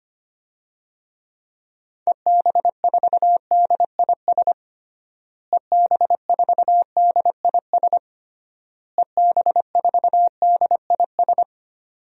Morse code, heard as a keyed tone, about 25 wpm, 700 Hz. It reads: EB4DIS EB4DIS EB4DIS
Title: DXCC-100 Call Signs - 6 Characters - Repeated 3x - 25wpm